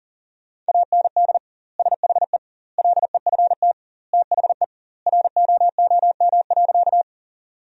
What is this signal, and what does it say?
Morse code, heard as a keyed tone, about 40 wpm, 700 Hz. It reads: AND SHE LEFT THE ROOM.
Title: 2x Word-Spacing - Sentences from Top 300 Words - 40wpm